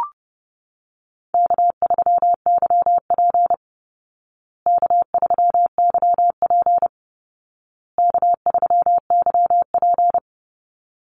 Morse code, heard as a keyed tone, about 30 wpm, 700 Hz. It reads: K3YP K3YP K3YP
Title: Call Signs Repeated 2x - 30wpm